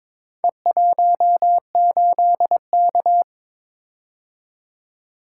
Morse code, heard as a keyed tone, about 22 wpm, 700 Hz. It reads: E18K
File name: practice / 22wpm / Four Letter-Number Combinations - 22wpm